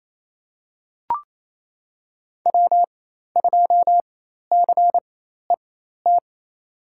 Morse code, heard as a keyed tone, about 28 wpm, spaced as for 13 wpm, 700 Hz. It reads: W2CET
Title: USA Call Signs - 5 Characters - 4x Character Spacing - 28wpm